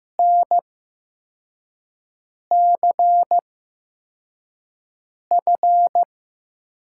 Morse code, heard as a keyed tone, about 15 wpm, spaced as for 4 wpm, 700 Hz. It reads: NCF
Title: USA Call Signs - 8x Character Spacing - 15wpm